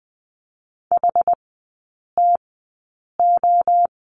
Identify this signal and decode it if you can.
Morse code, heard as a keyed tone, about 20 wpm, spaced as for 8 wpm, 700 Hz. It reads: HTO
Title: 100 Lesson 17e - Letter u - Three Characters 20wpm